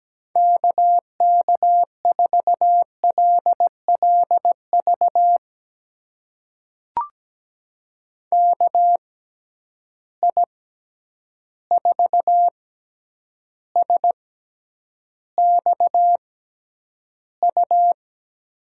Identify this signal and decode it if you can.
Morse code, heard as a keyed tone, about 17 wpm, spaced as for 9 wpm, 700 Hz. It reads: KK4LLV KI4SXU
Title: USA Call Signs - 6 Characters - 6x Character Spacing - 17wpm